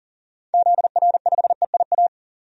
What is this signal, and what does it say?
Morse code, heard as a keyed tone, about 40 wpm, 700 Hz. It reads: ZR5EIA
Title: DXCC-100 Call Signs - 6 Characters - 40wpm